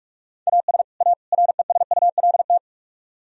Morse code, heard as a keyed tone, about 61 wpm, spaced as for 53 wpm, 700 Hz. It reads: AS A RESULT
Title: Speed-Racing - Sets of 3 Words 45wpm